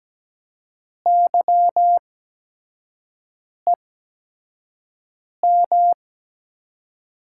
Morse code, heard as a keyed tone, about 17 wpm, spaced as for 5 wpm, 700 Hz. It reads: YEM
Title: USA Call Signs - 8x Character Spacing - 17wpm